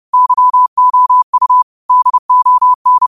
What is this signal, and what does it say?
Morse code, heard as a keyed tone, about 30 wpm, 1000 Hz. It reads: YOU DON
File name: qotdfiles / 30wpm